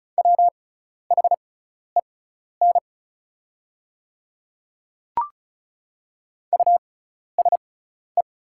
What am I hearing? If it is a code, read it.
Morse code, heard as a keyed tone, about 36 wpm, spaced as for 12 wpm, 700 Hz. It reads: WHEN USE
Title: 6x Character Spacing - Top 100 Words - 35wpm